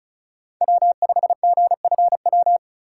Morse code, heard as a keyed tone, about 34 wpm, 700 Hz. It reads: W5GFW
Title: USA Call Signs - 5 Characters - 4x Character Spacing - 35wpm